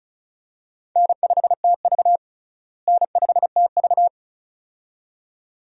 Morse code, heard as a keyed tone, about 36 wpm, 700 Hz. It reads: N5TV N5TV
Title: USA Call Signs - 4 Characters - Repeated 2x - 35wpm